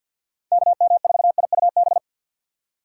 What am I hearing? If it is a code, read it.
Morse code, heard as a keyed tone, about 50 wpm, 700 Hz. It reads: KM4IUB
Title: USA Call Signs - 6x Character Spacing - 50wpm